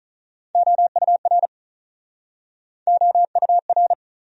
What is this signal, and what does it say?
Morse code, heard as a keyed tone, about 45 wpm, 700 Hz. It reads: OUR OUR